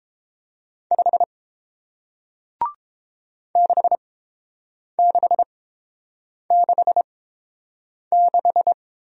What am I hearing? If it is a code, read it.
Morse code, heard as a keyed tone, about 32 wpm, spaced as for 8 wpm, 700 Hz. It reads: 5 6666